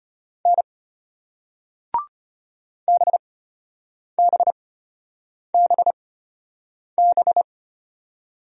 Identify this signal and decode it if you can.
Morse code, heard as a keyed tone, about 33 wpm, spaced as for 8 wpm, 700 Hz. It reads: N BBBB